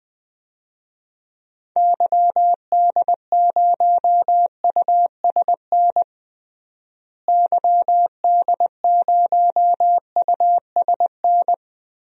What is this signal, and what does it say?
Morse code, heard as a keyed tone, about 20 wpm, 700 Hz. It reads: YD0USN YD0USN
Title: DXCC-100 Call Signs - 6 Characters - Repeated 3x - 20wpm